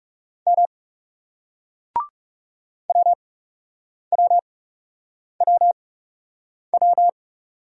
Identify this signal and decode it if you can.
Morse code, heard as a keyed tone, about 40 wpm, spaced as for 8 wpm, 700 Hz. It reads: M WWWW